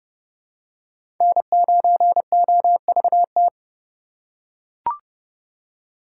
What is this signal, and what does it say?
Morse code, heard as a keyed tone, about 30 wpm, 700 Hz. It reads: N9OVT
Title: USA Call Signs - 5 Characters - 2x Character Spacing - 30wpm